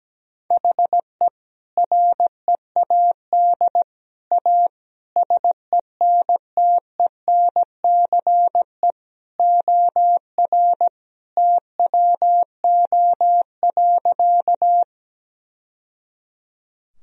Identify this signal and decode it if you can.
Morse code, heard as a keyed tone, about 17 wpm, 700 Hz. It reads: HE READ A SENTENCE OR TWO.
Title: Sentences from Top 1000 Words - Review Part 1 - 17wpm